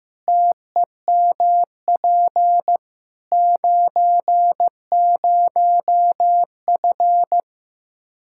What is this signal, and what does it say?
Morse code, heard as a keyed tone, about 15 wpm, 700 Hz. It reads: TEMP 90F
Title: Practice QSO - Exchange 2 - Info - 15wpm